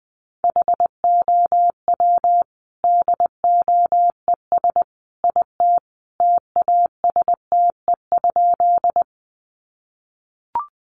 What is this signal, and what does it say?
Morse code, heard as a keyed tone, about 20 wpm, 700 Hz. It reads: HOW DOES IT TASTE?
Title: Sentences from Top 1100 Words 20wpm